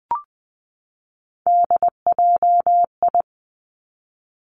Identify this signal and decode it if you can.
Morse code, heard as a keyed tone, about 20 wpm, 700 Hz. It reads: DJI
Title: Three Challenging Letter-Number Combinations 20wpm